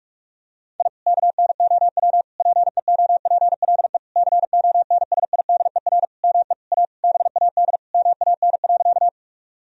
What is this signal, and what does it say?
Morse code, heard as a keyed tone, about 46 wpm, 700 Hz. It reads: I KNOW PEOPLE CONSIDER ME A BAD MAN.
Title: Sentences from Top 1000 Words - Review Part 4 - 45wpm